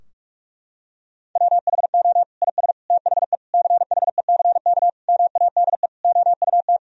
Morse code, heard as a keyed tone, about 41 wpm, 700 Hz. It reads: WHO IS THE CHECK MADE OUT
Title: Sentences from Top 1000 Words - Review Part 3 - 45wpm